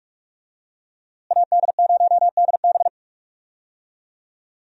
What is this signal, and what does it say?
Morse code, heard as a keyed tone, about 42 wpm, 700 Hz. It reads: AD0DB
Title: USA Call Signs - 5 Characters - Repeated 2x - 45wpm